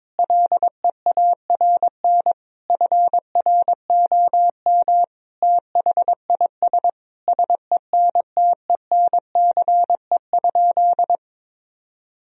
Morse code, heard as a keyed tone, about 21 wpm, 700 Hz. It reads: LEARN FROM THIS SENTENCE?
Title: Sentences from Top 1000 Words - Review Part 1 - 22wpm